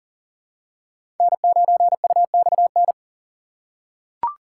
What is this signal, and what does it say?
Morse code, heard as a keyed tone, about 40 wpm, 700 Hz. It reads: N9UXN
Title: USA Call Signs - 5 Characters - 3x Character Spacing - 40wpm